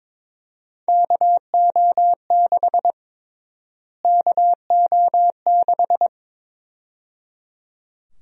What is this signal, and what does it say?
Morse code, heard as a keyed tone, about 22 wpm, 700 Hz. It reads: KO6 KO6
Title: USA Call Sign Prefixes - Repeated 2x - 22wpm